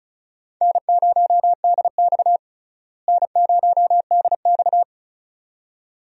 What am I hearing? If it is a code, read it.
Morse code, heard as a keyed tone, about 34 wpm, 700 Hz. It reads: N0DX N0DX
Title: USA Call Signs - 4 Characters - Repeated 2x - 35wpm